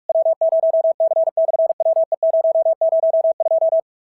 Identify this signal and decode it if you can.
Morse code, heard as a keyed tone, about 46 wpm, 650 Hz. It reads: W0C/WE002
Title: SOTA QSO Practice 45wpm